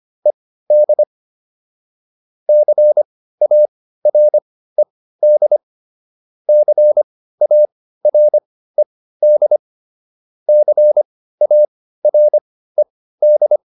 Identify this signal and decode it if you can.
Morse code, heard as a keyed tone, about 25 wpm, spaced as for 15 wpm, 600 Hz. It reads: ED CARED CARED CARED